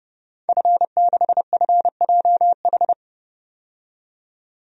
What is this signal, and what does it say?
Morse code, heard as a keyed tone, about 30 wpm, 700 Hz. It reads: F6FJH
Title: DXCC-100 Call Signs - 5 Characters - 30wpm